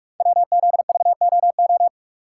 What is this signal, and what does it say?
Morse code, heard as a keyed tone, about 44 wpm, 700 Hz. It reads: WZVOO
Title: Code Groups - 5 Letter:Numbers - 45wpm